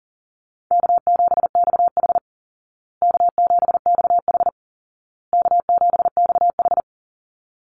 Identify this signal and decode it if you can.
Morse code, heard as a keyed tone, about 40 wpm, 700 Hz. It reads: K7XH K7XH K7XH
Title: Easy Call Signs Repeated 3x - 40wpm